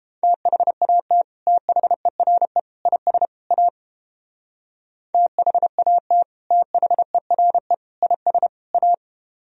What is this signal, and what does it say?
Morse code, heard as a keyed tone, about 30 wpm, 700 Hz. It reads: THAT THERE IS A THAT THERE IS A